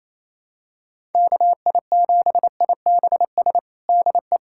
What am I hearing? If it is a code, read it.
Morse code, heard as a keyed tone, about 28 wpm, 700 Hz. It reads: KI7IBS DE
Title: USA Callsign DE Callsign 28wpm